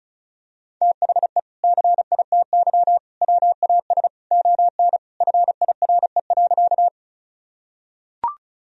Morse code, heard as a keyed tone, about 34 wpm, 700 Hz. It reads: THE CITY WAS ON FIRE.
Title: Sentences from Top 1000 Words - Review Part 1 - 35wpm